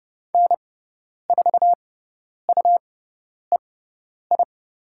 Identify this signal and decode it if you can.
Morse code, heard as a keyed tone, about 30 wpm, spaced as for 10 wpm, 700 Hz. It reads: N4UEI
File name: Lesson 18g - Letter m - Call Signs 30wpm